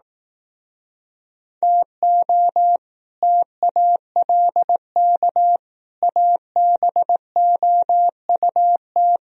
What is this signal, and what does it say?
Morse code, heard as a keyed tone, about 18 wpm, 700 Hz. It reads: TO TALK ABOUT
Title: Speed-Racing - Sets of 4 Words 15wpm